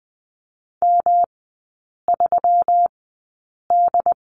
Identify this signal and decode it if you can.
Morse code, heard as a keyed tone, about 20 wpm, spaced as for 8 wpm, 690 Hz. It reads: M3D
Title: 159 Lesson 26e - Letter g - Three Characters 20wpm